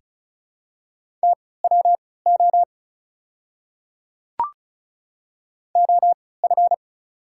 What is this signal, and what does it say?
Morse code, heard as a keyed tone, about 36 wpm, spaced as for 20 wpm, 690 Hz. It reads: TWO OF
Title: 3x Character Spacing - Top 100 Words - 35wpm